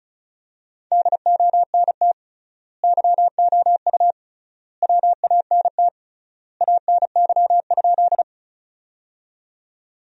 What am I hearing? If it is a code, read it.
Morse code, heard as a keyed tone, about 36 wpm, 700 Hz. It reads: DONT YOU WANT ANY?
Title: 3x Word-Spacing - Sentences from Top 300 Words - 35wpm